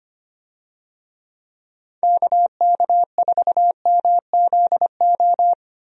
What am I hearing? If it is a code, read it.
Morse code, heard as a keyed tone, about 25 wpm, 700 Hz. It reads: KK4MZO
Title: USA Call Signs - 6 Characters - 25wpm